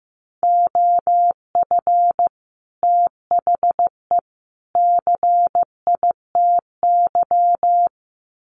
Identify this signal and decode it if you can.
Morse code, heard as a keyed tone, about 15 wpm, 700 Hz. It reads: OF THE CITY
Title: Sets of 3 Words 15wpm